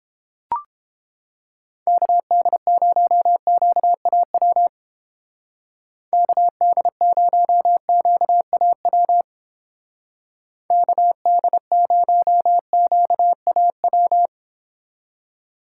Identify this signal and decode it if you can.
Morse code, heard as a keyed tone, about 30 wpm, 700 Hz. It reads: KD0QAW KD0QAW KD0QAW